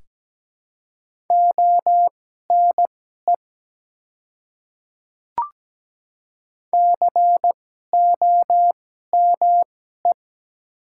Morse code, heard as a keyed tone, about 17 wpm, spaced as for 12 wpm, 700 Hz. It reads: ONE COME
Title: 2x Character Spacing - Top 100 Words - 17wpm